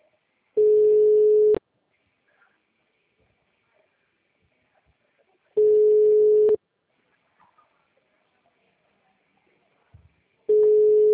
Ringback tone (Europe, 425 Hz).